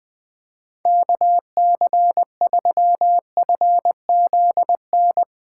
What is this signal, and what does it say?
Morse code, heard as a keyed tone, about 21 wpm, 700 Hz. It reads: KC3FZN
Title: Speed Racing - USA 2x3 Call Signs 20wpm